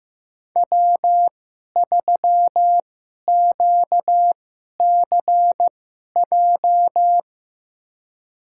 Morse code, heard as a keyed tone, about 15 wpm, spaced as for 11 wpm, 700 Hz. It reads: W3QCJ
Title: USA Call Signs - 2x Character Spacing - 15wpm